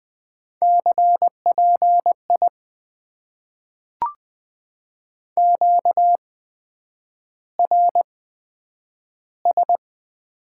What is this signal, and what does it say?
Morse code, heard as a keyed tone, about 20 wpm, spaced as for 9 wpm, 700 Hz. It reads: CPI QRS
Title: Top 90 QSO Elements - 8x Character Spacing - 20wpm